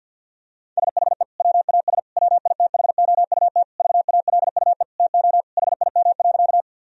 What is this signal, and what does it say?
Morse code, heard as a keyed tone, about 46 wpm, 700 Hz. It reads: SHE WAS WITHOUT VALUE TO HIM.
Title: Sentences from Top 1000 Words - Review Part 3 - 50wpm